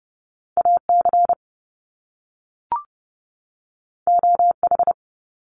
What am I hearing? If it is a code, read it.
Morse code, heard as a keyed tone, about 30 wpm, 700 Hz. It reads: AC OH
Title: Top Two Letter Combinations in Words 30wpm